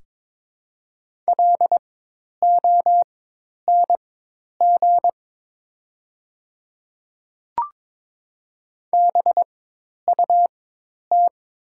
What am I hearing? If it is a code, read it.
Morse code, heard as a keyed tone, about 23 wpm, spaced as for 10 wpm, 700 Hz. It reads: LONG BUT
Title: 4x Character Spacing - Top 500 Words - 22wpm